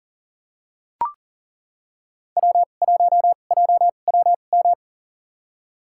Morse code, heard as a keyed tone, about 40 wpm, spaced as for 29 wpm, 690 Hz. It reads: W1JWM
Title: USA Call Signs - 2x Character Spacing - 40wpm